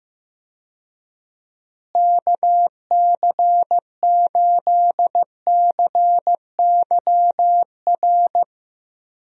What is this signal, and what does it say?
Morse code, heard as a keyed tone, about 15 wpm, 700 Hz. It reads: KC8CYR
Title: USA Call Signs - 6 Characters - Repeated 3x - 15wpm